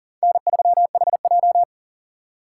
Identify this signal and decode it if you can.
Morse code, heard as a keyed tone, about 40 wpm, 700 Hz. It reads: N3HJ
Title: USA Call Signs - 4x Character Spacing - 40wpm